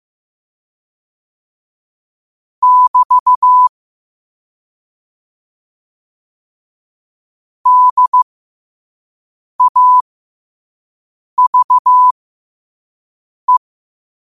Morse code, heard as a keyed tone, about 15 wpm, spaced as for 5 wpm, 1000 Hz. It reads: <BT> DAVE
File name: qotdfiles / 5wpm